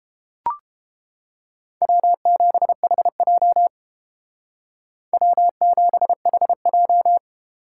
Morse code, heard as a keyed tone, about 31 wpm, 700 Hz. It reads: W7HJ W7HJ